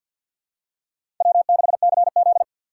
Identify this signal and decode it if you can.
Morse code, heard as a keyed tone, about 50 wpm, 700 Hz. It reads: W6CZ
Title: USA Call Signs - 4 Characters - 2x Character Spacing - 50wpm